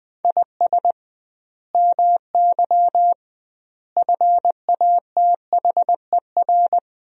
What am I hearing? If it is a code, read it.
Morse code, heard as a keyed tone, about 20 wpm, 700 Hz. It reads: IS MY FATHER
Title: 2x Word-Spacing - Sentences from Top 300 Words - 20wpm